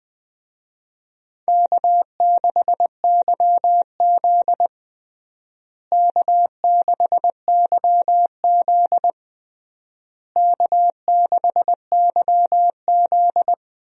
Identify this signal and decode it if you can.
Morse code, heard as a keyed tone, about 20 wpm, 700 Hz. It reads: K6YZ K6YZ K6YZ
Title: USA Call Signs - 4 Characters - Repeated 3x - 20wpm